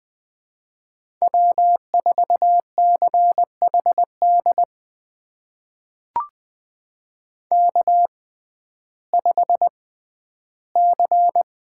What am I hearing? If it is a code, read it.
Morse code, heard as a keyed tone, about 20 wpm, spaced as for 12 wpm, 700 Hz. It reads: W4CHD K5C
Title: USA Call Signs - 5 Characters - 6x Character Spacing - 20wpm